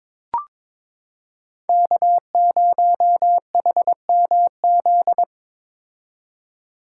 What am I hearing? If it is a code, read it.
Morse code, heard as a keyed tone, about 22 wpm, 700 Hz. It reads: K0HMZ
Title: K1USN SST QSO Practice - 22wpm